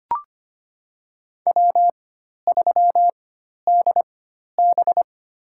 Morse code, heard as a keyed tone, about 25 wpm, 700 Hz. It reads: W 3 D B
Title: USA Call Signs - 4x Character Spacing - 25wpm